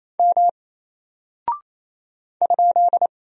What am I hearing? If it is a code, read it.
Morse code, heard as a keyed tone, about 27 wpm, 700 Hz. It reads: M ?
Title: Single Character (Letter:Number:Punctuation:Prosign) 28wpm